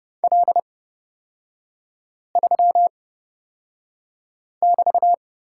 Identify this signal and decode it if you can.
Morse code, heard as a keyed tone, about 30 wpm, spaced as for 5 wpm, 700 Hz. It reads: L3<BT>
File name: Lesson 36e - BT - Three Characters 30wpm